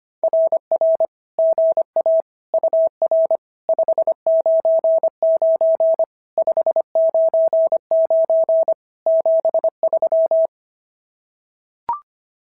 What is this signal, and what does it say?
Morse code, heard as a keyed tone, about 25 wpm, 650 Hz. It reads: RR GA UR 599 599 73
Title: SOTA QSO Practice 25wpm